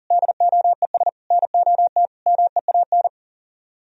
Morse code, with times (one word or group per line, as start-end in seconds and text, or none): 0.1-1.1 DOES
1.3-2.1 NOT
2.3-3.1 MEAN